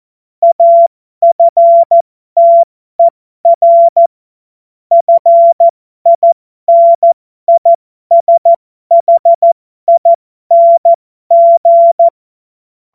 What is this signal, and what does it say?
Morse code, heard as a keyed tone, about 14 wpm, spaced as for 12 wpm, 680 Hz. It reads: AFTER FINISHING
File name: Morsekurs / Witze